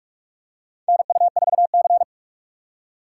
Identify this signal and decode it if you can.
Morse code, heard as a keyed tone, about 44 wpm, 700 Hz. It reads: NU4C
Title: USA Call Signs - 4 Characters - 4x Character Spacing - 45wpm